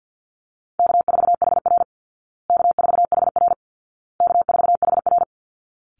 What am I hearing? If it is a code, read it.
Morse code, heard as a keyed tone, about 50 wpm, 700 Hz. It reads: K4HR K4HR K4HR
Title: Easy Call Signs Repeated 3x - 50wpm